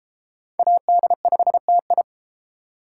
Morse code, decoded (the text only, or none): AD5TI